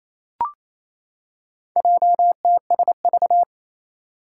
Morse code, heard as a keyed tone, about 28 wpm, 700 Hz. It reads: JTSV
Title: Four Challenging Letter-Number Combinations 28wpm